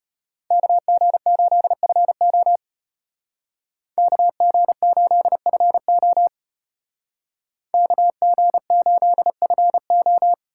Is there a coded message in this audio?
Yes: Morse code from 0.5 to 10.3 s, heard as a keyed tone, about 34 wpm, 700 Hz. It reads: KG8FO KG8FO KG8FO